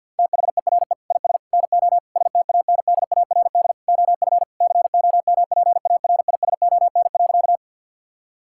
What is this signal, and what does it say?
Morse code, heard as a keyed tone, about 50 wpm, 700 Hz. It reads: THERE IS NO STANDARD OF COMPARISON.
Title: Sentences from Top 1600 Words 50wpm